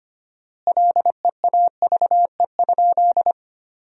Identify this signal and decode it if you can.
Morse code, heard as a keyed tone, about 25 wpm, 700 Hz. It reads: LEAVE?
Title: 4x Word-Spacing - Sentences from Top 400 Words - 25wpm